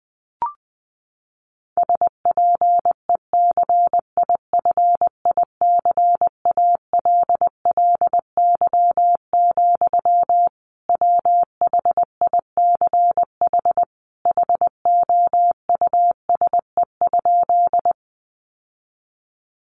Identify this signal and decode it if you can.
Morse code, heard as a keyed tone, about 20 wpm, 700 Hz. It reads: SPECIFICALLY, WHICH HOUSE?
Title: Sentences from Top 1100 Words 20wpm